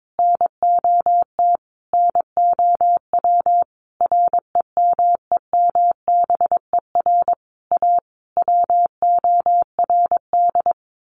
Morse code, heard as a keyed tone, about 22 wpm, 700 Hz. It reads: NOT NOW REMEMBER A WORD
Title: Sentences from Top 1200 Words 22wpm